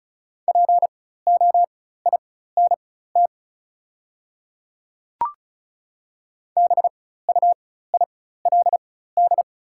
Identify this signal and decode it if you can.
Morse code, heard as a keyed tone, about 34 wpm, spaced as for 16 wpm, 690 Hz. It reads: POINT BUILD
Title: 4x Character Spacing - Top 200 Words - 35wpm